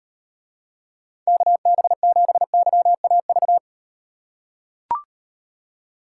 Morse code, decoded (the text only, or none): KB7YAV